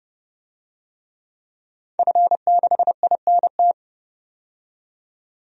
Morse code, heard as a keyed tone, about 30 wpm, 700 Hz. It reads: F6INT
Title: DXCC-100 Call Signs - 5 Characters - 30wpm